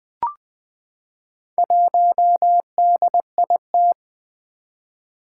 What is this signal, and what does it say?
Morse code, heard as a keyed tone, about 20 wpm, 700 Hz. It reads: 1DIT
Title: Four Challenging Letter-Number Combinations 20wpm